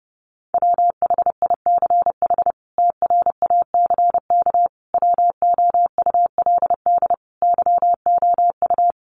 Morse code, heard as a keyed tone, about 30 wpm, 700 Hz. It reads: WHICH TRACK WOULD YOU
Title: Sentences from Top 1100 Words 30wpm